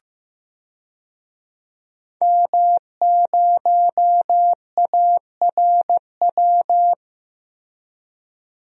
Morse code, heard as a keyed tone, about 15 wpm, 700 Hz. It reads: M0ARW
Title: DXCC-100 Call Signs - 5 Characters - 15wpm